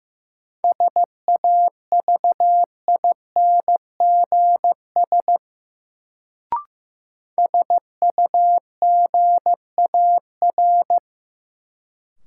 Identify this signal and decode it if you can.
Morse code, heard as a keyed tone, about 15 wpm, 700 Hz. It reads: SAVINGS SUGAR
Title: Sentences from Top 1200 Words - New Word Drill - 15wpm